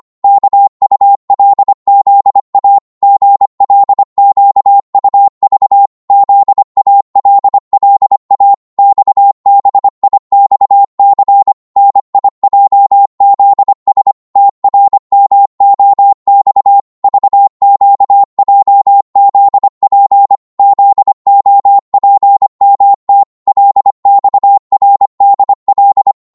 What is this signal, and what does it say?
Morse code, heard as a keyed tone, about 25 wpm, 800 Hz. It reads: KULZA GLQUV ZALLA XBIXC NIJZS TRMOX VQJZP ZOPMT LXRDL